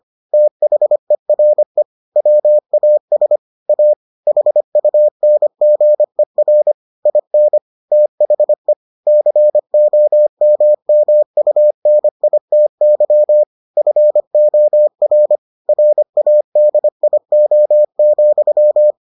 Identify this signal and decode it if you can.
Morse code, heard as a keyed tone, about 25 wpm, 600 Hz. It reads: THERE WAS A HUNGER IN THE COMMUNITY FOR RADIO,